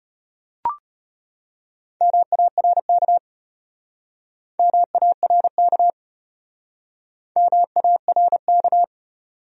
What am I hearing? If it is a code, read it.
Morse code, heard as a keyed tone, about 33 wpm, 700 Hz. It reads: MARK MARK MARK